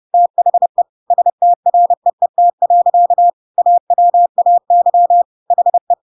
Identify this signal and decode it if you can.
Morse code, heard as a keyed tone, about 30 wpm, 700 Hz. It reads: THE STREET. AWAY HE
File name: Pinocchio / Pinocchio ch3 30 30 ik1hge